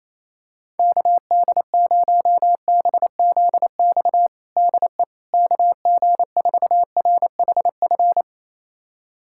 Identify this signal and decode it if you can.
Morse code, heard as a keyed tone, about 28 wpm, 700 Hz. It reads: KD0BZX DE KG4RHF
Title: USA Callsign DE Callsign - 6 Characters - 28wpm